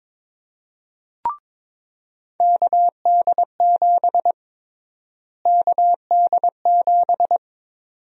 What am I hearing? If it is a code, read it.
Morse code, heard as a keyed tone, about 22 wpm, 700 Hz. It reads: KD7 KD7
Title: USA Call Sign Prefixes - Repeated 2x - 22wpm